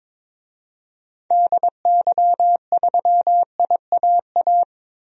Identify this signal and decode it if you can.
Morse code, heard as a keyed tone, about 22 wpm, 700 Hz. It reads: DY3IAA
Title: DXCC-100 Call Signs - 6 Characters - Repeated 3x - 22wpm